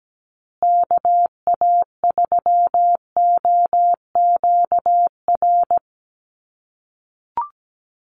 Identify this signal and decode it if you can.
Morse code, heard as a keyed tone, about 17 wpm, 700 Hz. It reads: KA3OQR
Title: USA Call Signs Repeated 3x 17wpm